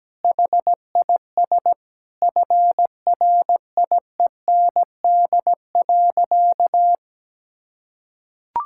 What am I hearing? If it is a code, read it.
Morse code, heard as a keyed tone, about 17 wpm, 700 Hz. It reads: HIS FRIEND.
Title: Sentences from Top 1000 Words - Review Part 5 - 17wpm